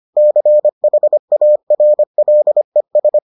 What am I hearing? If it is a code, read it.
Morse code, heard as a keyed tone, about 25 wpm, 600 Hz. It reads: CHARLES